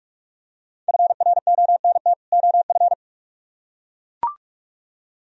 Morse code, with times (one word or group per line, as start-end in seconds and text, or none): 0.9-2.1 FRONT
2.3-2.9 OF